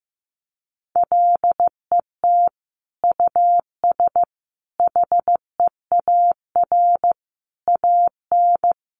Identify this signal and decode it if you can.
Morse code, heard as a keyed tone, about 15 wpm, 700 Hz. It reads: LET US HEAR AN
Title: Sentences from Top 900 Words 15wpm